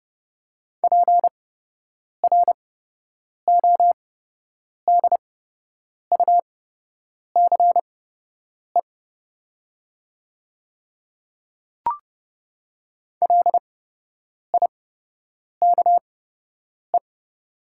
Morse code, heard as a keyed tone, about 30 wpm, spaced as for 8 wpm, 700 Hz. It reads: PRODUCE LIKE
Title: 8x Character Spacing - Top 500 Words - 30wpm